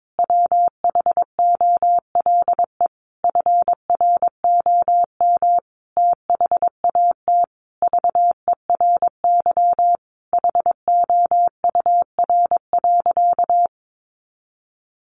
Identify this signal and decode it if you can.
Morse code, heard as a keyed tone, about 22 wpm, 700 Hz. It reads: WHOLE FROM THAT VERY HOUR.
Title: Sentences from Top 1500 Words 22wpm